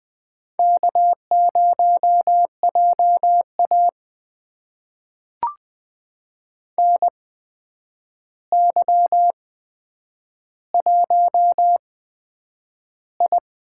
Morse code, heard as a keyed tone, about 20 wpm, spaced as for 9 wpm, 700 Hz. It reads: K0JA NY1I
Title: USA Call Signs - 4 Characters - 8x Character Spacing - 20wpm